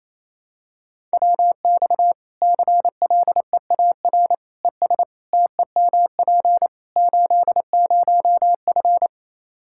Morse code, heard as a keyed tone, about 27 wpm, 700 Hz. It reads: WX CLEAR ES TEMP 80F
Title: K1USN SST QSO Practice 28wpm